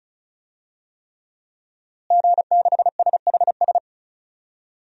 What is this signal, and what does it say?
Morse code, heard as a keyed tone, about 34 wpm, 700 Hz. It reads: G6SHS
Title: DXCC-100 Call Signs - 5 Characters - Repeated 3x - 35wpm